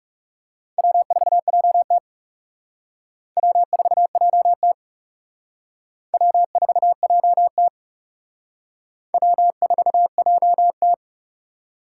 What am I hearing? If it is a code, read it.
Morse code, heard as a keyed tone, about 40 wpm, 700 Hz. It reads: W4JT W4JT W4JT W4JT